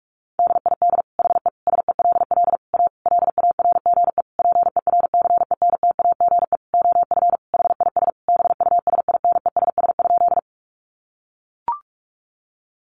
Morse code, heard as a keyed tone, about 44 wpm, 700 Hz. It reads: DID HE SELL A LARGE PERCENTAGE OF HIS BUSINESS?
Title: Sentences from Top 1300 Words 45wpm